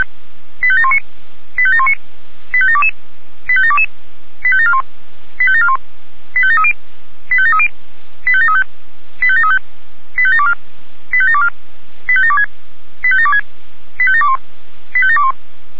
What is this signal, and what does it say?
Five-tone selective call (ZVEI), heard as quick runs of five tones, tones of about 70 ms. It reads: 85719 85719 85722 85722 85741 85741 85730 85730 85735 85735 85724 85724 85726 85726 85721 85721
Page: /464/SDR/ZVEI/